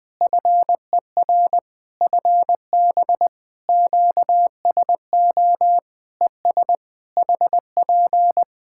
Morse code, heard as a keyed tone, about 21 wpm, 700 Hz. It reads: FER FB QSO ES HP
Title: K1USN SST QSO Practice 20wpm